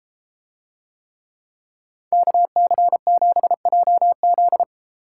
Morse code, heard as a keyed tone, about 34 wpm, 700 Hz. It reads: KC7JZ